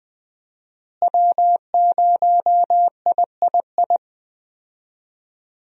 Morse code, heard as a keyed tone, about 21 wpm, 700 Hz. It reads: W0III
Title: USA Call Signs - 5 Characters - Repeated 3x - 20wpm